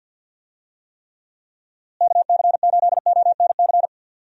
Morse code, heard as a keyed tone, about 52 wpm, 700 Hz. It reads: KC8ONC
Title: USA Call Signs - 6 Characters - Repeated 3x - 50wpm